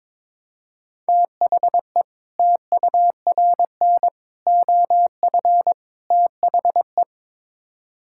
Morse code, heard as a keyed tone, about 22 wpm, 700 Hz. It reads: THE TURN OF THE
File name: practice / 22wpm / Build A Phrase - 5-Word Phrases - 22wpm